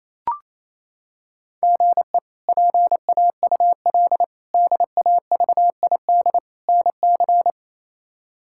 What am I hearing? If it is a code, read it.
Morse code, heard as a keyed tone, about 28 wpm, 700 Hz. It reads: GE PAUL DAVID NC
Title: K1USN SST QSO Practice - 28wpm